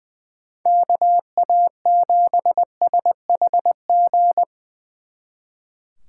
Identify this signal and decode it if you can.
Morse code, heard as a keyed tone, about 22 wpm, 700 Hz. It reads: KA7SHG
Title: Speed Racing - USA 2x3 Call Signs 20wpm